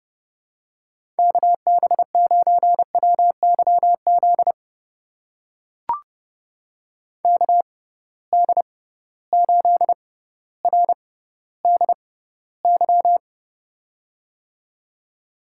Morse code, heard as a keyed tone, about 32 wpm, spaced as for 16 wpm, 700 Hz. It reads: KB9WYZ KD8RDY
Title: USA Call Signs - 6 Characters - 6x Character Spacing - 30wpm